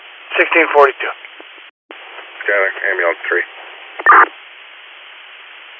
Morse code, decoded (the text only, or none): none